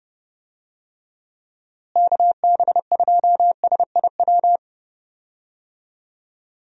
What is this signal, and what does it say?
Morse code, heard as a keyed tone, about 31 wpm, 700 Hz. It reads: KB2SIW
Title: USA Call Signs - 6 Characters - Repeated 3x - 30wpm